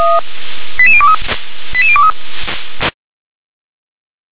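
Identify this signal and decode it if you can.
Five-tone selective call (ZVEI), heard as quick runs of five tones, tones of about 70 ms. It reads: 88023 88023
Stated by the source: ZVEI five-tone selective call received on SDR.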